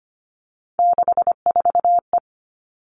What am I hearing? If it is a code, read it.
Morse code, heard as a keyed tone, about 25 wpm, 700 Hz. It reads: 64E
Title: All Cut Numbers - 3-Digit 25wpm